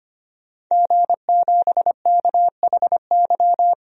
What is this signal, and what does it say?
Morse code, heard as a keyed tone, about 25 wpm, 700 Hz. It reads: G7KHY
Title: DXCC-100 Call Signs - 5 Characters - 25wpm